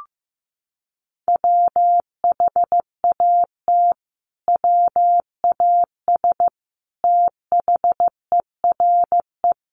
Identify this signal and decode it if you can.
Morse code, heard as a keyed tone, about 15 wpm, 700 Hz. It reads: WHAT WAS THERE
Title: Sentences from Top 1100 Words 15wpm